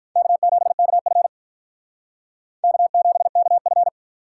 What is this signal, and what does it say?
Morse code, heard as a keyed tone, about 44 wpm, 700 Hz. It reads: K7KF K7KF